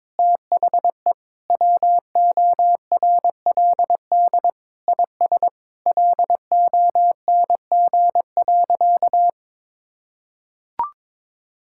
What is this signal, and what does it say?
Morse code, heard as a keyed tone, about 22 wpm, 700 Hz. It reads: THE WORLD IS LONG.
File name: practice / 22wpm / Sentences with 3-Word Phrases - 22wpm